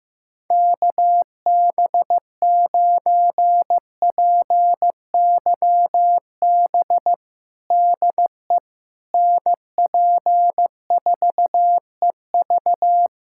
Morse code, heard as a keyed tone, about 15 wpm, 700 Hz. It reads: KB9PYB DE NP4EV
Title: USA Callsign DE Callsign 15wpm